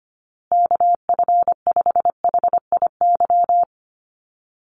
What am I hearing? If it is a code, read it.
Morse code, heard as a keyed tone, about 25 wpm, 700 Hz. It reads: KF5HIY
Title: USA Call Signs Repeated 3x 25wpm